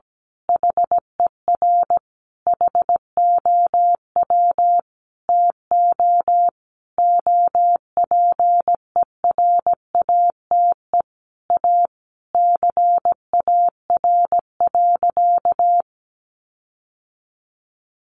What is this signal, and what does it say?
Morse code, heard as a keyed tone, about 17 wpm, 700 Hz. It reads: HER HOW TO OPERATE A CAR.
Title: Sentences from Top 1500 Words 17wpm